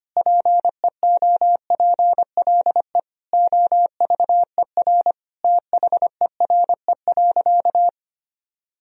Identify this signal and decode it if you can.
Morse code, heard as a keyed tone, about 25 wpm, 700 Hz. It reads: PEOPLE OVER THERE.
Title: Sentences from Top 1000 Words - Review Part 1 - 25wpm